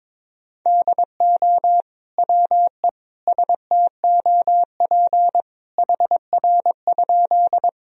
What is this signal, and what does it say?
Morse code, heard as a keyed tone, about 21 wpm, 700 Hz. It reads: DO WE STOP HR?
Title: Sentences from Top 300 Words and Abbreviations 22wpm